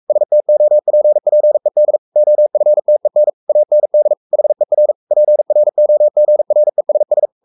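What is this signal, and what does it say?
Morse code, heard as a keyed tone, about 44 wpm, 600 Hz. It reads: STOPPED OFTEN AND HER PROGRESS
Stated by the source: Morse code at 43 wpm sending Ken Follett's novel Never in English, from CW med Gnister.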